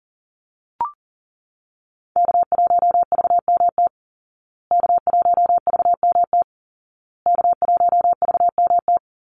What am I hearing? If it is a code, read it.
Morse code, heard as a keyed tone, about 40 wpm, 700 Hz. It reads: K1VMT K1VMT K1VMT
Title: Call Signs Repeated 2x - 40wpm